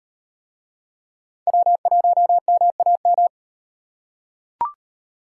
Morse code, heard as a keyed tone, about 38 wpm, 700 Hz. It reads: W1MAM